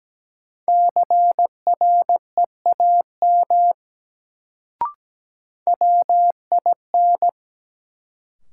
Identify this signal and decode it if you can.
Morse code, heard as a keyed tone, about 17 wpm, 700 Hz. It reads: CREAM WIN
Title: Sentences from Top 1500 Words - New Word Drill - 17wpm